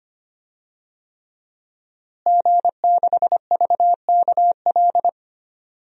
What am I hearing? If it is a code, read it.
Morse code, heard as a keyed tone, about 25 wpm, 700 Hz. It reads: G6VKL
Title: DXCC-100 Call Signs - 5 Characters - Repeated 3x - 25wpm